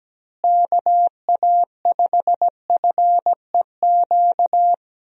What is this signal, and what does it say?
Morse code, heard as a keyed tone, about 17 wpm, 700 Hz. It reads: KA5FEQ
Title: USA Call Signs - 6 Characters - Repeated 3x - 17wpm